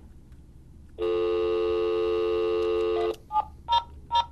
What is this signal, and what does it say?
Dial tone (North America), 350 + 440 Hz, with noise.
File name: fios dv call cannot be completed as dialed